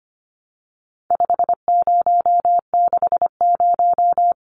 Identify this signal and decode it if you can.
Morse code, heard as a keyed tone, about 25 wpm, 700 Hz. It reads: 5060
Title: Numbers - 4-Digit 25wpm